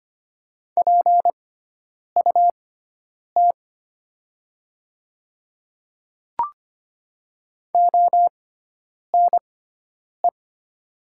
Morse code, heard as a keyed tone, about 25 wpm, spaced as for 9 wpm, 690 Hz. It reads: PUT ONE
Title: 6x Character Spacing - Top 200 Words - 25wpm